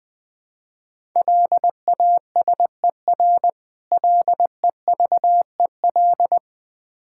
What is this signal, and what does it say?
Morse code, heard as a keyed tone, about 21 wpm, 690 Hz. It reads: LASER LEVEL
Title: Difficult Words with Hints - 20wpm